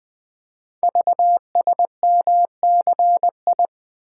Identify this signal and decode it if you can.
Morse code, heard as a keyed tone, about 21 wpm, 700 Hz. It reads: VSMCI
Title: Code Groups - 5 Letter:Numbers - 20wpm